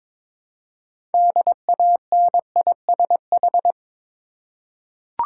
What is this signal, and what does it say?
Morse code, heard as a keyed tone, about 23 wpm, 690 Hz. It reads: DANISH
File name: Difficult Words with Hints - 22wpm